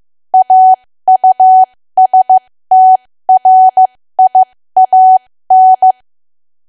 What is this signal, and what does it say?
Morse code, heard as a keyed tone, about 15 wpm, spaced as for 13 wpm, 750 Hz. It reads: AUSTRIAN
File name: cwa / W1AW-Level2 / 13WPM